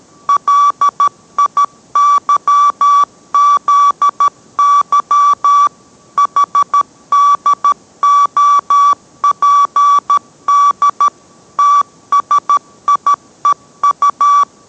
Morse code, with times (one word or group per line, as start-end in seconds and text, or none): 0.3-5.7 LIYZY
6.2-11.1 HDOPD
11.6-14.4 TSIEU